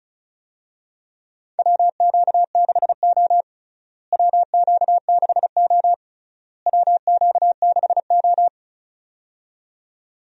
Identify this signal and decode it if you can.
Morse code, heard as a keyed tone, about 36 wpm, 700 Hz. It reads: WQ6O WQ6O WQ6O
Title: USA Call Signs - 4 Characters - Repeated 3x - 35wpm